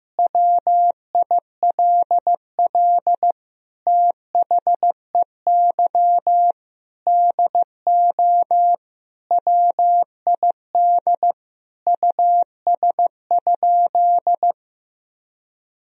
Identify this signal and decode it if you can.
Morse code, heard as a keyed tone, about 15 wpm, 700 Hz. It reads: WILL THEY DO WID US?
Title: Sentences from Top 200 Words and Abbreviations 15wpm